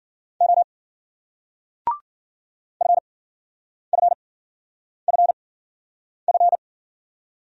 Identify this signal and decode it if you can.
Morse code, heard as a keyed tone, about 56 wpm, spaced as for 9 wpm, 700 Hz. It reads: O FFFF